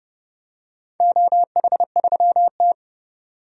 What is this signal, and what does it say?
Morse code, heard as a keyed tone, about 30 wpm, 700 Hz. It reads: OH3T
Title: DXCC-100 Call Signs - 4 Characters - 30wpm